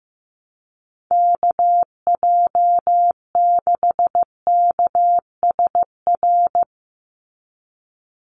Morse code, heard as a keyed tone, about 15 wpm, 700 Hz. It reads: KJ6KSR
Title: USA Call Signs 15wpm